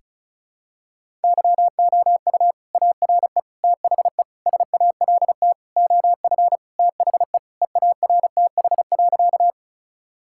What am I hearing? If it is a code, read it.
Morse code, heard as a keyed tone, about 34 wpm, 700 Hz. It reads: YOU ARE THE SALT OF THE EARTH.
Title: Sentences from Top 1000 Words - Review Part 5 - 35wpm